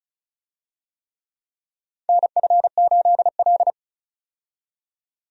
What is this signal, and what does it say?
Morse code, heard as a keyed tone, about 36 wpm, 700 Hz. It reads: NF8L
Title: USA Call Signs - 4 Characters - 35wpm